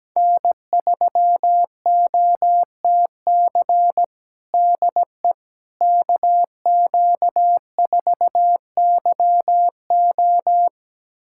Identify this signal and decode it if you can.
Morse code, heard as a keyed tone, about 17 wpm, 700 Hz. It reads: N3OTC DE KQ4YO
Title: USA Callsign DE Callsign - 5 Characters - 17wpm